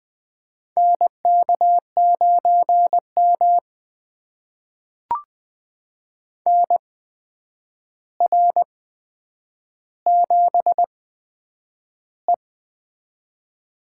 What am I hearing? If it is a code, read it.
Morse code, heard as a keyed tone, about 20 wpm, spaced as for 9 wpm, 700 Hz. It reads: NK9M NR7E
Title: USA Call Signs - 8x Character Spacing - 20wpm